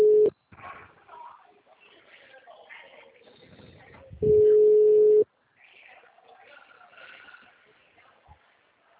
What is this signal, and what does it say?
Ringback tone (Europe, 425 Hz), with speech.